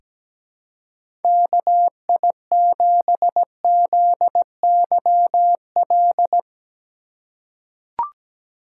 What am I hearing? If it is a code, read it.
Morse code, heard as a keyed tone, about 17 wpm, 700 Hz. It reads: KI7ZYL
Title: USA Call Signs - 6 Characters - 3x Character Spacing - 17wpm